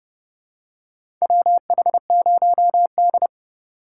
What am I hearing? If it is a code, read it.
Morse code, heard as a keyed tone, about 30 wpm, 700 Hz. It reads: WH0D
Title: USA Call Signs - 4 Characters - 3x Character Spacing - 30wpm